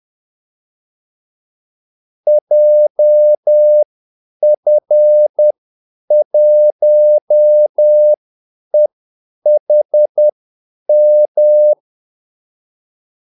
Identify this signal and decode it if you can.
Morse code, heard as a keyed tone, about 10 wpm, spaced as for 8 wpm, 600 Hz. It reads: JF1EHM
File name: Callsigns-10-8wpm